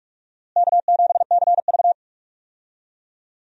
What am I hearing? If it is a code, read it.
Morse code, heard as a keyed tone, about 46 wpm, 700 Hz. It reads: K7CV
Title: USA Call Signs - 4 Characters - Repeated 3x - 45wpm